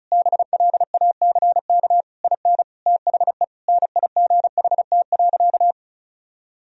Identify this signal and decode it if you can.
Morse code, heard as a keyed tone, about 34 wpm, 700 Hz. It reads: BLACK IN THE NIGHT.
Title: Sentences from Top 1000 Words - Review Part 4 - 35wpm